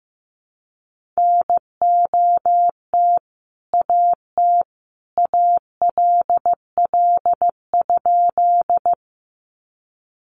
Morse code, heard as a keyed tone, about 15 wpm, 700 Hz. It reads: NOT AT ALL?
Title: Sentences from Top 600 Words 15wpm